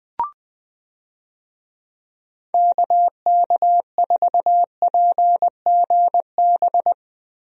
Morse code, heard as a keyed tone, about 20 wpm, 700 Hz. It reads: KK4PGB
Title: USA Call Signs - 6 Characters - Repeated 2x - 20wpm